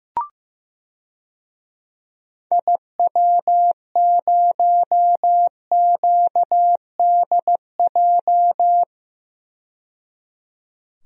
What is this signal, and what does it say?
Morse code, heard as a keyed tone, about 15 wpm, 700 Hz. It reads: IW0QDJ
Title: DXCC-100 Call Signs - 6 Characters - 15wpm